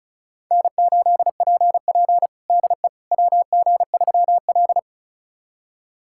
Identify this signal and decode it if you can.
Morse code, heard as a keyed tone, about 36 wpm, 700 Hz. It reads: N8PP DE WG3L
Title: USA Callsign DE Callsign - 4 Characters - 35wpm